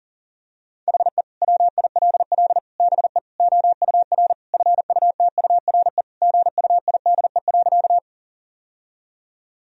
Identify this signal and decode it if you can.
Morse code, heard as a keyed tone, about 40 wpm, 700 Hz. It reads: HE WILL BE OUR FUTURE GUIDE.
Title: Sentences from Top 1000 Words - Review Part 3 - 40wpm